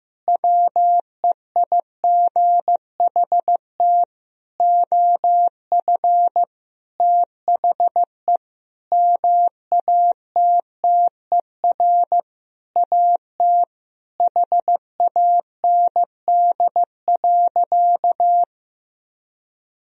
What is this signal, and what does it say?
Morse code, heard as a keyed tone, about 15 wpm, 700 Hz. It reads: WEIGHT OF THE MATTER AT HAND.
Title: Sentences from Top 1000 Words - Review Part 3 - 15wpm